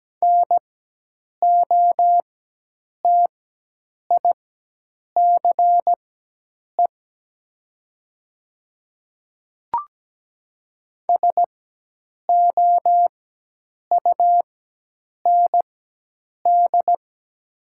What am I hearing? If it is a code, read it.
Morse code, heard as a keyed tone, about 17 wpm, spaced as for 8 wpm, 700 Hz. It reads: NOTICE SOUND
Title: 4x Character Spacing - Top 500 Words - 17wpm